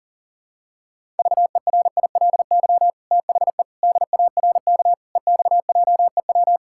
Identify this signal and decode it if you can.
Morse code, heard as a keyed tone, about 40 wpm, 700 Hz. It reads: VERILY THE DARK EXJEW
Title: Pangram Challenge v2 - 40wpm